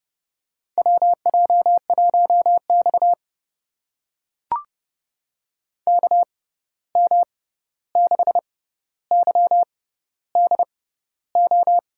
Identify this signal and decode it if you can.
Morse code, heard as a keyed tone, about 30 wpm, spaced as for 14 wpm, 700 Hz. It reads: WJ1X KM6YDO
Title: USA Call Signs - 6x Character Spacing - 30wpm